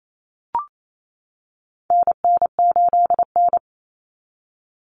Morse code, heard as a keyed tone, about 28 wpm, 700 Hz. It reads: NN8N
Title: Common Cut Numbers - 4-Digit 28wpm